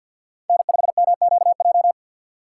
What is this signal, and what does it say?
Morse code, heard as a keyed tone, about 48 wpm, 700 Hz. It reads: N5MQJ
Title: USA Call Signs - 5 Characters - 3x Character Spacing - 50wpm